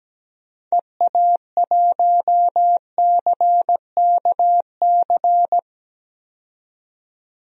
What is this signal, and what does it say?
Morse code, heard as a keyed tone, about 17 wpm, 700 Hz. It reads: EA1CKC
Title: DXCC-100 Call Signs - 6 Characters - 17wpm